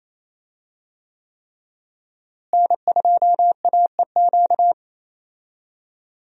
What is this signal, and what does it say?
Morse code, heard as a keyed tone, about 28 wpm, 700 Hz. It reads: N2AEQ
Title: USA Call Signs - 5 Characters - Repeated 3x - 28wpm